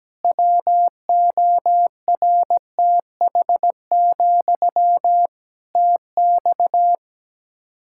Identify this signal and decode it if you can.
Morse code, heard as a keyed tone, about 17 wpm, 700 Hz. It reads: WORTH, TX
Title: Practice QSO - Exchange 1 - Beginning - 17wpm